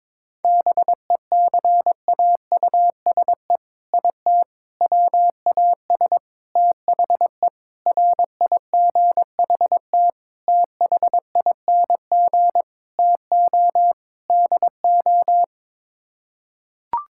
Morse code, heard as a keyed tone, about 22 wpm, 700 Hz. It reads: BECAUSE IT WAS THE RIGHT THING TO DO
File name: practice / 22wpm / Sets of 8 Words - 22wpm